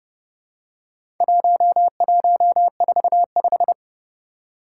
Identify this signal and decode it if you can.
Morse code, heard as a keyed tone, about 30 wpm, 700 Hz. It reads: 1145
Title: Four Challenging Letter-Number Combinations 30wpm